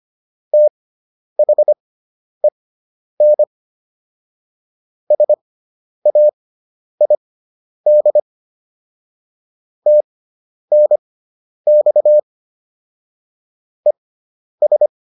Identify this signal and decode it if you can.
Morse code, heard as a keyed tone, about 25 wpm, spaced as for 10 wpm, 600 Hz. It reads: THEN SAID TNX ES